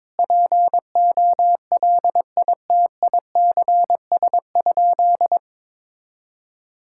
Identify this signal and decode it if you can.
Morse code, heard as a keyed tone, about 22 wpm, 700 Hz. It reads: POLITICS?
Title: Sentences from Top 1000 Words - Review Part 6 - 22wpm